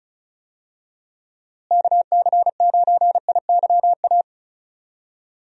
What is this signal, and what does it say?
Morse code, heard as a keyed tone, about 36 wpm, 700 Hz. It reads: KC9IYA
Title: USA Call Signs - 6 Characters - Repeated 3x - 35wpm